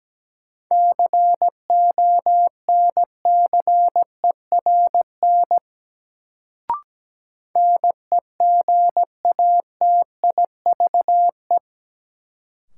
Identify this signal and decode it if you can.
Morse code, heard as a keyed tone, about 17 wpm, 700 Hz. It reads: CONCERN NEGATIVE
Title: Sentences from Top 1000 Words - New Word Drill - 17wpm